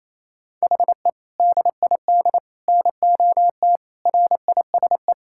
Morse code, heard as a keyed tone, about 27 wpm, 700 Hz. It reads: HE DID NOT RISE
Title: Sentences from Top 1000 Words - Review Part 6 - 28wpm